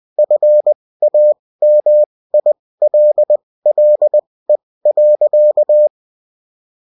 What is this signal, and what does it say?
Morse code, heard as a keyed tone, about 20 wpm, spaced as for 16 wpm, 600 Hz. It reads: FAMILLE.